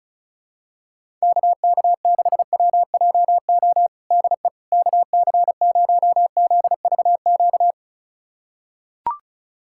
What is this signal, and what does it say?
Morse code, heard as a keyed tone, about 34 wpm, 700 Hz. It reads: KK6WJO DE KC0ZVQ
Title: USA Callsign DE Callsign 35wpm